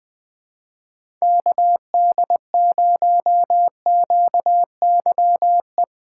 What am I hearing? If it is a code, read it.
Morse code, heard as a keyed tone, about 21 wpm, 700 Hz. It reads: KD0QYE
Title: USA Call Signs - 6 Characters - 2x Character Spacing - 20wpm